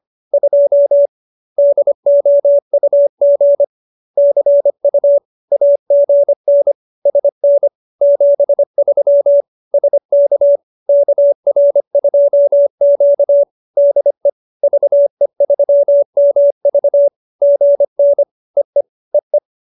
Morse code, heard as a keyed tone, about 25 wpm, 570 Hz. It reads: 2 DOUG CU AGN SN 73 SK KR2Q DE VE3MV GN EE EE